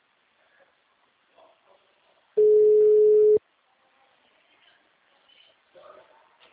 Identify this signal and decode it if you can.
Ringback tone (Europe, 425 Hz).
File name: out- -2003- - - .9525